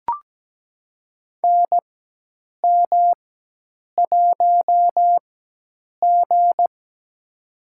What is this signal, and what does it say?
Morse code, heard as a keyed tone, about 17 wpm, spaced as for 8 wpm, 700 Hz. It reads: NM1G
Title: USA Call Signs - 4 Characters - 4x Character Spacing - 17wpm